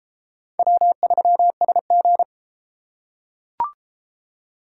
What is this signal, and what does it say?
Morse code, heard as a keyed tone, about 31 wpm, 700 Hz. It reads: W3SG